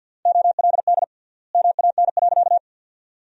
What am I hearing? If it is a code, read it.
Morse code, heard as a keyed tone, about 50 wpm, 700 Hz. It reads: OLD MAN.
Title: 3x Word-Spacing - Sentences from Top 200 Words - 50wpm